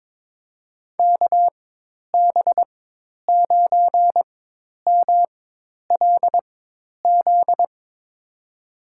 Morse code, heard as a keyed tone, about 21 wpm, spaced as for 10 wpm, 700 Hz. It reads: KB9MLZ
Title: USA Call Signs - 6 Characters - 4x Character Spacing - 22wpm